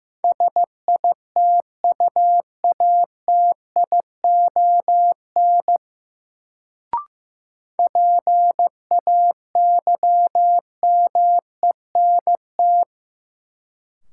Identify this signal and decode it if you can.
Morse code, heard as a keyed tone, about 15 wpm, 700 Hz. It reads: SITUATION PAYMENT
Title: Sentences from Top 1200 Words - New Word Drill - 15wpm